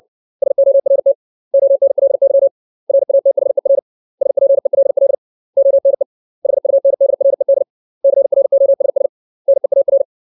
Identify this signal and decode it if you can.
Morse code, heard as a keyed tone, about 60 wpm, 550 Hz. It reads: SORT ONLY RATHER SPELL ONE HUNDRED CROSS NEAR
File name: 632 words / 632 Words 60WPM